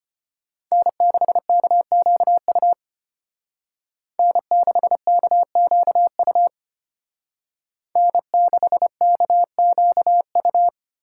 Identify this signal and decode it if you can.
Morse code, heard as a keyed tone, about 32 wpm, 700 Hz. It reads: N6KQU N6KQU N6KQU